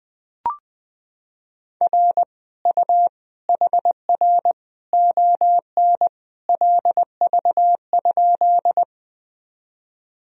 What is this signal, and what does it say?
Morse code, heard as a keyed tone, about 21 wpm, 700 Hz. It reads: R U HR ON LV?
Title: Sentences from Top 400 Words and Abbreviations 20wpm